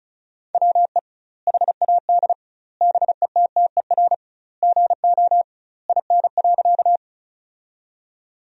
Morse code, heard as a keyed tone, about 36 wpm, 700 Hz. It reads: WE HAD BETTER GO IN.
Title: 2x Word-Spacing - Sentences from Top 400 Words - 35wpm